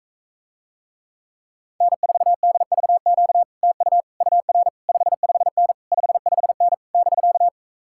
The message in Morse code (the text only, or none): N4DVQ TU UR 55N 55N <BK>